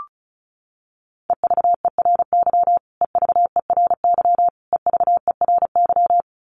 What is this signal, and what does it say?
Morse code, heard as a keyed tone, about 34 wpm, 700 Hz. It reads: EVERY EVERY EVERY
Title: Top 300 Words Repeated 35wpm